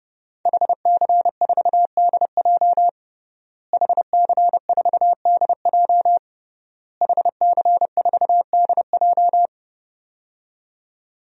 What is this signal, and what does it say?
Morse code, heard as a keyed tone, about 30 wpm, 700 Hz. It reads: HC4DJ HC4DJ HC4DJ
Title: DXCC-100 Call Signs - 5 Characters - Repeated 3x - 30wpm